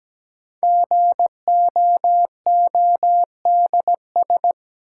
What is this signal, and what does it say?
Morse code, heard as a keyed tone, about 17 wpm, 700 Hz. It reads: GOODS
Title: Sentences from Top 1000 Words - Review Part 6 - New Word Drill - 17wpm